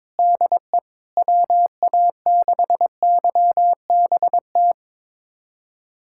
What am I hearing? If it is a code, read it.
Morse code, heard as a keyed tone, about 22 wpm, 700 Hz. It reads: DE WA6YBT
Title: USA Callsign DE Callsign 22wpm